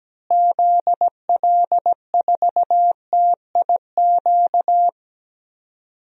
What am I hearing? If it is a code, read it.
Morse code, heard as a keyed tone, about 17 wpm, 700 Hz. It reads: ZL4TIQ
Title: DXCC-100 Call Signs - 6 Characters - Repeated 3x - 17wpm